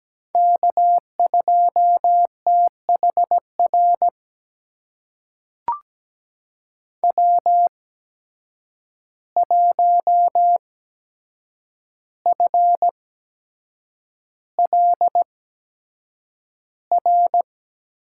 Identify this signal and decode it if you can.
Morse code, heard as a keyed tone, about 17 wpm, spaced as for 7 wpm, 700 Hz. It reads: K2THR W1FLR
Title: USA Call Signs - 5 Characters - 8x Character Spacing - 17wpm